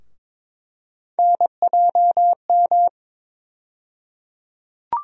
Morse code, heard as a keyed tone, about 22 wpm, 700 Hz. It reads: NJM
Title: USA Call Sign Postfixes - 3 Letters 22wpm